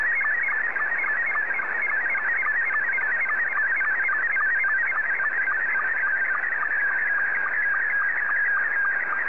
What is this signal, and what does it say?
SSTV (slow-scan television), Scottie 2.
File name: sstv / S2-JA3QJQ